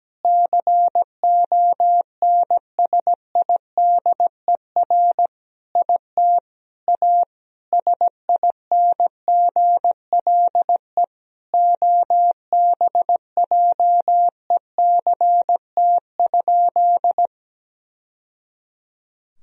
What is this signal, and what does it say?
Morse code, heard as a keyed tone, about 17 wpm, 700 Hz. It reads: CONSIDER IT A SINGLE OBJECT?
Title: Sentences from Top 1000 Words - Review Part 4 - 17wpm